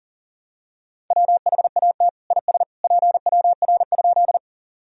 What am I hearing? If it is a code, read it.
Morse code, heard as a keyed tone, about 40 wpm, 700 Hz. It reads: WHAT IS PWR?
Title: Sentences from Top 500 Words and Abbreviations 40wpm